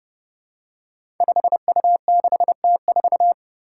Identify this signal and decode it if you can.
Morse code, heard as a keyed tone, about 30 wpm, 700 Hz. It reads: 5U6T4
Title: Five Challenging Letter-Number Combinations 30wpm